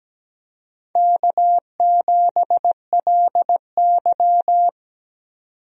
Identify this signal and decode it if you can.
Morse code, heard as a keyed tone, about 17 wpm, 700 Hz. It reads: K7LY
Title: USA Call Signs - 4 Characters - 17wpm